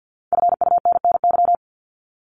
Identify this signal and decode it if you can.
Morse code, heard as a keyed tone, about 50 wpm, 700 Hz. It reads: FUNNY
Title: Top 1751-2000 Words 50wpm